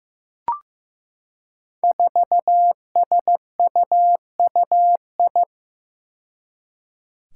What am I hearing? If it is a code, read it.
Morse code, heard as a keyed tone, about 15 wpm, 700 Hz. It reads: 4SUUI
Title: Five Challenging Letter-Number Combinations 15wpm